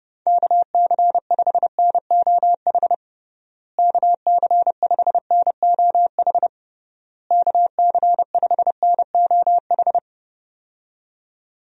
Morse code, heard as a keyed tone, about 30 wpm, 700 Hz. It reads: KC5NOH KC5NOH KC5NOH
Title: USA Call Signs - 6 Characters - Repeated 3x - 30wpm